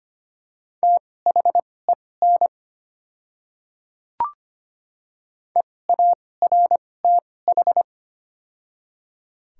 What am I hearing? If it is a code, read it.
Morse code, heard as a keyed tone, about 25 wpm, spaced as for 18 wpm, 700 Hz. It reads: THEN EARTH
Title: 2x Character Spacing - Top 200 Words - 25wpm